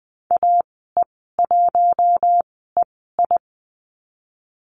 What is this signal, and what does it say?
Morse code, heard as a keyed tone, about 20 wpm, spaced as for 15 wpm, 700 Hz. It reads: AE1EI
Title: 076 Lesson 13g - Letter L - Call Signs 20wpm